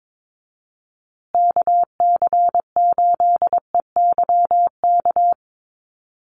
Morse code, heard as a keyed tone, about 22 wpm, 700 Hz. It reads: KC8EYK
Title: USA Call Signs Repeated 2x 22wpm